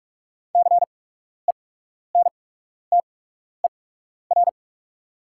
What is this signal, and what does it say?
Morse code, heard as a keyed tone, about 44 wpm, spaced as for 12 wpm, 700 Hz. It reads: CENTER
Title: 8x Character Spacing - Top 500 Words - 45wpm